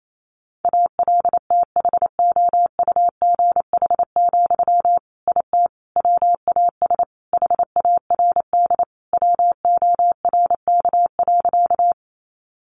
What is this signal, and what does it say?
Morse code, heard as a keyed tone, about 28 wpm, 700 Hz. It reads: ALTHOUGH, IT WAS HARD WORK.
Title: Sentences from Top 600 Words 28wpm